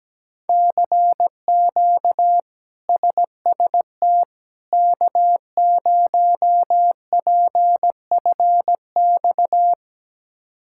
Morse code, heard as a keyed tone, about 17 wpm, 700 Hz. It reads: CQ SST K0PFX
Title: K1USN SST QSO Practice - 17wpm